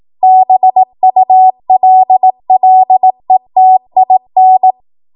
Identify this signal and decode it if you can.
Morse code, heard as a keyed tone, about 18 wpm, 750 Hz. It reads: BULLETIN
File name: Archive / 18 WPM / 18WPM